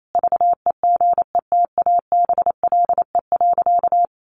Morse code, heard as a keyed tone, about 28 wpm, 700 Hz. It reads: VEGETABLE.
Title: Sentences from Top 1300 Words 28wpm